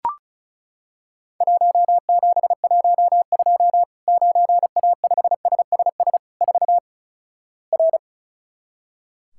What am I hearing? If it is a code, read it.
Morse code, heard as a keyed tone, about 34 wpm, 700 Hz. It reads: 1712 9A5SSS 4 R
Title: Worked All Europe DX Contest - QTC Exchange - 35wpm